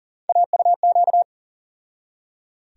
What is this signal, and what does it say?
Morse code, heard as a keyed tone, about 40 wpm, 700 Hz. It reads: AUQ
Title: USA Call Sign Postfixes - 3 Letters 40wpm